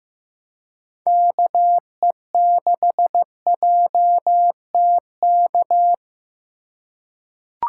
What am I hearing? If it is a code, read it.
Morse code, heard as a keyed tone, about 15 wpm, 700 Hz. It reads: KE6JTK
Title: USA Call Signs - 6 Characters - 4x Character Spacing - 15wpm